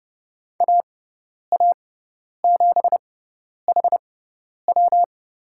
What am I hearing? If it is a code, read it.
Morse code, heard as a keyed tone, about 30 wpm, spaced as for 10 wpm, 700 Hz. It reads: AA7HW
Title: USA Call Signs - 5 Characters - 6x Character Spacing - 30wpm